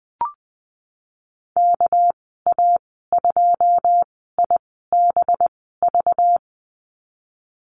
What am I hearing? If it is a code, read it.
Morse code, heard as a keyed tone, about 19 wpm, spaced as for 14 wpm, 700 Hz. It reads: KA2IBV
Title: 245 Lesson 39g - Letter z - Call Signs 20wpm